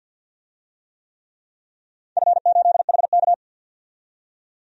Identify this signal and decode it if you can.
Morse code, heard as a keyed tone, about 52 wpm, 700 Hz. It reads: F8HK
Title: DXCC-100 Call Signs - 4 Characters - 50wpm